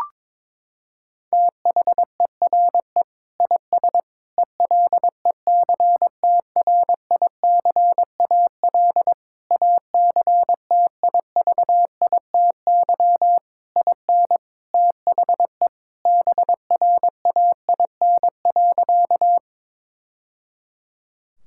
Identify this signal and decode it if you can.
Morse code, heard as a keyed tone, about 22 wpm, 700 Hz. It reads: THERE IS ELECTRICAL ACTIVITY IN THE BRAIN.
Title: Sentences from Top 1000 Words - Review Part 6 - 22wpm